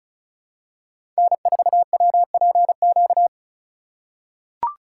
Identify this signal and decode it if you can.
Morse code, heard as a keyed tone, about 36 wpm, 700 Hz. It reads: N4WPQ
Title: USA Call Signs - 5 Characters - 2x Character Spacing - 35wpm